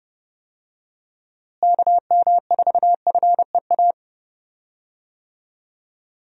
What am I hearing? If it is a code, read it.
Morse code, heard as a keyed tone, about 30 wpm, 700 Hz. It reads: KM4FEA
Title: USA Call Signs - 6 Characters - Repeated 3x - 30wpm